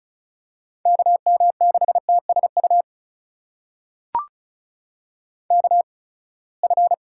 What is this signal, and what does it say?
Morse code, heard as a keyed tone, about 36 wpm, spaced as for 24 wpm, 700 Hz. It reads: KM6TSU KF
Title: USA Call Signs - 6 Characters - 8x Character Spacing - 35wpm